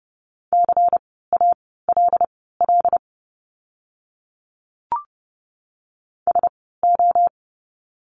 Morse code, heard as a keyed tone, about 30 wpm, spaced as for 17 wpm, 700 Hz. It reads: CALL SO
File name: course / 30:10wpm / 230 Lesson 37f - Letter x - Words 30wpm